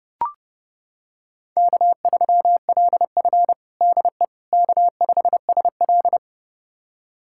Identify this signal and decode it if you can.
Morse code, heard as a keyed tone, about 32 wpm, 700 Hz. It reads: K3LF DE K5SL
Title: USA Callsign DE Callsign - 4 Characters - 30wpm